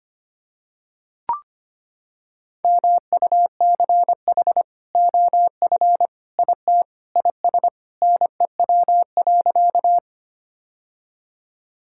Morse code, heard as a keyed tone, about 25 wpm, 700 Hz. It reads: MUCH OF IT IS NEW.
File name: practice / 25wpm / Sentences with 2-Word Phrases - 25wpm